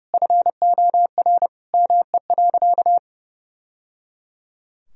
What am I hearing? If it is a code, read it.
Morse code, heard as a keyed tone, about 30 wpm, 700 Hz. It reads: FOR ME.
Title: Sentences from Top 1000 Words - Review Part 3 - 30wpm